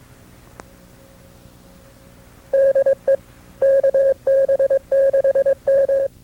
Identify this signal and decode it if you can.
Morse code, heard as a keyed tone, about 22 wpm, 540 Hz. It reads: DE KB6M
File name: FreeFiles / CW-practice / KB6MT-tape2-side2